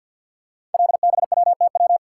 Morse code, heard as a keyed tone, about 51 wpm, 700 Hz. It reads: LBWTW
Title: Code Groups - 5 Letter:Numbers - 50wpm